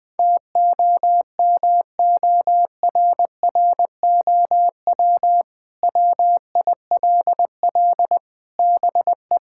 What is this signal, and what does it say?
Morse code, heard as a keyed tone, about 20 wpm, 700 Hz. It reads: TOMORROW WILL BE
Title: Sentences with 2-Word Phrases - Spoken Beginning - 20wpm